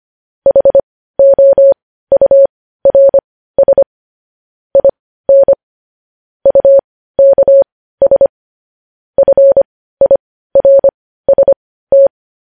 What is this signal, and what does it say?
Morse code, heard as a keyed tone, about 25 wpm, spaced as for 15 wpm, 550 Hz. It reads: HOURS IN UKS FIRST
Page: news-headlines-in-morse-code-at-15-wpm